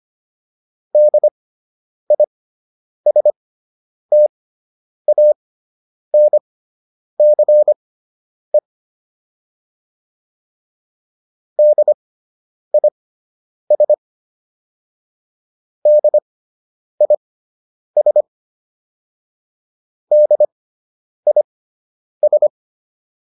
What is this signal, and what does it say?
Morse code, heard as a keyed tone, about 25 wpm, spaced as for 9 wpm, 600 Hz. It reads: DISTANCE DIS DIS DIS